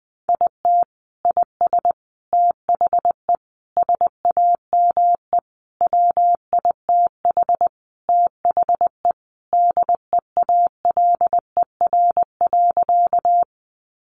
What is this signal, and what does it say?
Morse code, heard as a keyed tone, about 20 wpm, 700 Hz. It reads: IT IS THE SAME WITH THE DEALER.
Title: Sentences from Top 1100 Words 20wpm